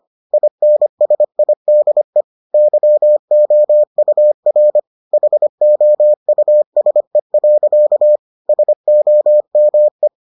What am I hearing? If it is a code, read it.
Morse code, heard as a keyed tone, about 25 wpm, 600 Hz. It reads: INSIDE YOUR HOUSE. SOME